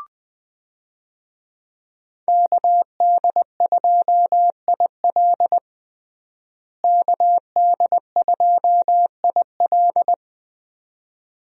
Morse code, heard as a keyed tone, about 20 wpm, 700 Hz. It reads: KD2IL KD2IL
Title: USA Call Signs - 5 Characters - Repeated 2x - 20wpm